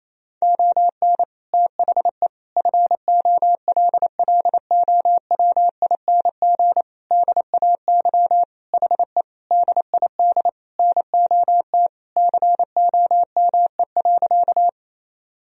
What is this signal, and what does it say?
Morse code, heard as a keyed tone, about 28 wpm, 700 Hz. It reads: ON THE FOLLOWING DAY HE DID NOT COME.
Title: Sentences from Top 1000 Words - Review Part 4 - 28wpm